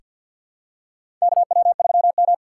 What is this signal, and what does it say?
Morse code, heard as a keyed tone, about 45 wpm, 700 Hz. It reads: KW3M
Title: USA Call Signs - 4 Characters - 3x Character Spacing - 50wpm